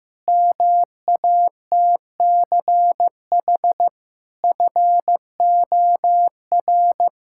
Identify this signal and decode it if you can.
Morse code, heard as a keyed tone, about 15 wpm, 700 Hz. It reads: MATCH FOR
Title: Sentences from Top 1600 Words 15wpm